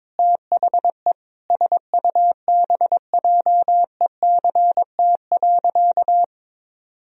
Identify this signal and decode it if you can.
Morse code, heard as a keyed tone, about 22 wpm, 700 Hz. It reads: THE SUBJECT.
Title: Sentences from Top 1000 Words - Review Part 1 - 22wpm